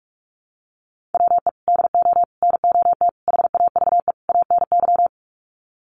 Morse code, heard as a keyed tone, about 44 wpm, 700 Hz. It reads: WE DO NOT HAVE ANY
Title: Sets of 5 Words 45wpm